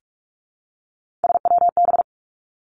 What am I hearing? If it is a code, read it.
Morse code, heard as a keyed tone, about 46 wpm, 700 Hz. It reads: SWB
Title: Three Challenging Letter-Number Combinations 45wpm